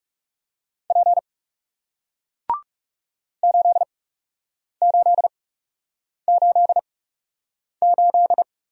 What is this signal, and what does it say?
Morse code, heard as a keyed tone, about 41 wpm, spaced as for 8 wpm, 700 Hz. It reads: P 8888